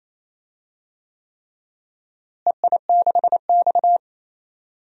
Morse code, heard as a keyed tone, about 28 wpm, 700 Hz. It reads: EI6X
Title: DXCC-100 Call Signs - 4 Characters - Repeated 3x - 28wpm